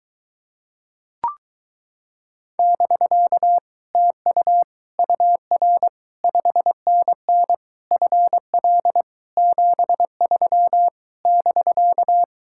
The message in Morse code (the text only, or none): <BK> TU UR 5NN FL 73 <BK>